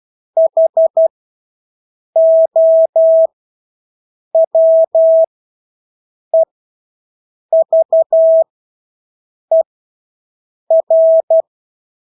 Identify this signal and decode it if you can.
Morse code, heard as a keyed tone, about 12 wpm, spaced as for 6 wpm, 650 Hz. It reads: HOWEVER
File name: Engelske ord 12-6 wpm